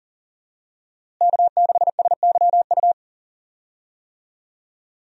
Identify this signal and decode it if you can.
Morse code, heard as a keyed tone, about 39 wpm, 700 Hz. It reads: K6SYU
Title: USA Call Signs - 5 Characters - Repeated 2x - 40wpm